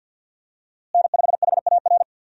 Morse code, heard as a keyed tone, about 50 wpm, 700 Hz. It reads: N5HAR
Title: USA Call Signs - 5 Characters - Repeated 3x - 50wpm